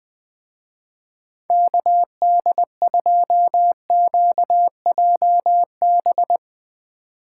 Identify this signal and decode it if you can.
Morse code, heard as a keyed tone, about 20 wpm, 700 Hz. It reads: KD2QJB
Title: Speed Racing - USA 2x3 Call Signs 17wpm